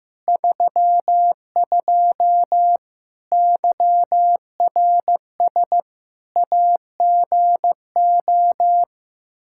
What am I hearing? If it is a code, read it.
Morse code, heard as a keyed tone, about 15 wpm, 700 Hz. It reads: 32 YRS AGO
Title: Practice QSO - Exchange 3 - More Info - 15wpm